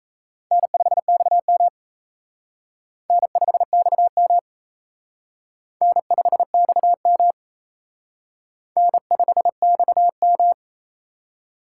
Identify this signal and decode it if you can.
Morse code, heard as a keyed tone, about 32 wpm, 700 Hz. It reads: N5XM N5XM N5XM N5XM
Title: Speed Racing - USA 1x2 Call Signs 28wpm